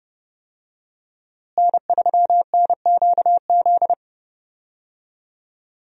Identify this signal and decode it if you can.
Morse code, heard as a keyed tone, about 30 wpm, 700 Hz. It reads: N3NQZ
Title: USA Call Signs - 5 Characters - Repeated 3x - 30wpm